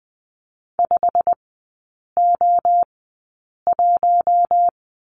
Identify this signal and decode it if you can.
Morse code, heard as a keyed tone, about 20 wpm, spaced as for 8 wpm, 700 Hz. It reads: 5O1
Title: 145 Lesson 24e - Letter y - Three Characters 20wpm